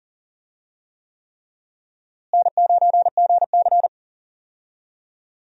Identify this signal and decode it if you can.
Morse code, heard as a keyed tone, about 39 wpm, 700 Hz. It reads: N9GC
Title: USA Call Signs - 4 Characters - Repeated 3x - 40wpm